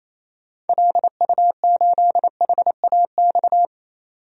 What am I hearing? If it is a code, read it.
Morse code, heard as a keyed tone, about 28 wpm, 700 Hz. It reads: LU8HAX
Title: DXCC-100 Call Signs - 6 Characters - 28wpm